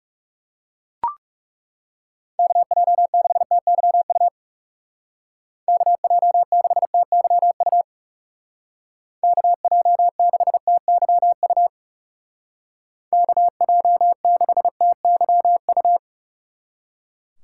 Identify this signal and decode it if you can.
Morse code, heard as a keyed tone, about 37 wpm, 700 Hz. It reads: KJ6TYU KJ6TYU KJ6TYU KJ6TYU
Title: Speed Racing - USA Call Signs 30wpm